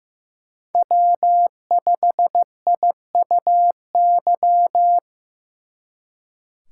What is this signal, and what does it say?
Morse code, heard as a keyed tone, about 18 wpm, 700 Hz. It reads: W5IUY
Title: Speed Racing - USA 1x3 Call Signs 15wpm